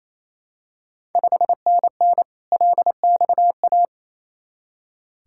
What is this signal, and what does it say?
Morse code, heard as a keyed tone, about 28 wpm, 700 Hz. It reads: 5NN LXA
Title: Ontario QSO Party Exchange 28wpm